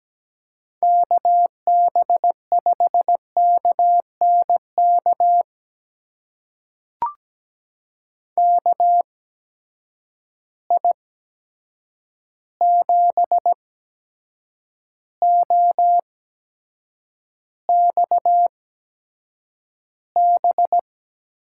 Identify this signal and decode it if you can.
Morse code, heard as a keyed tone, about 17 wpm, spaced as for 7 wpm, 700 Hz. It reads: KB5KNK KI7OXB
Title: USA Call Signs - 6 Characters - 8x Character Spacing - 17wpm